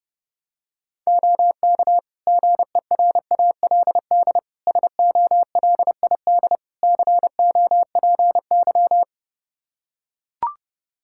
Morse code, heard as a keyed tone, about 30 wpm, 700 Hz. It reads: OK GERALD SOLID COPY
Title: Practice QSO - Exchange 3 - More Info - 30wpm